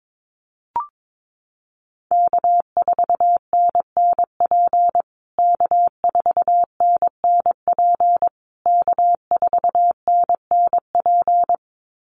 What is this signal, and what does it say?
Morse code, heard as a keyed tone, about 22 wpm, 700 Hz. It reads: K4NNP K4NNP K4NNP
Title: USA Call Signs Repeated 3x 22wpm